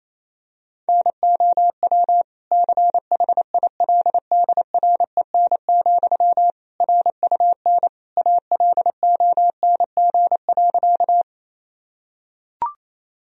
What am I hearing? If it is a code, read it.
Morse code, heard as a keyed tone, about 28 wpm, 700 Hz. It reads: NOW CHILDREN, RUN ALONG.
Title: Sentences from Top 1000 Words - Review Part 1 - 28wpm